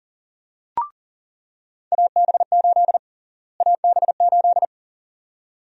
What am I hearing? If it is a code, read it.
Morse code, heard as a keyed tone, about 40 wpm, 700 Hz. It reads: AB8 AB8
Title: USA Call Sign Prefixes - Repeated 2x - 40wpm